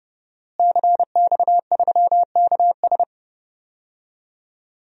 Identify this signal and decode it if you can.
Morse code, heard as a keyed tone, about 29 wpm, 700 Hz. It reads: CX3KS
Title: DXCC-100 Call Signs - 5 Characters - Repeated 2x - 30wpm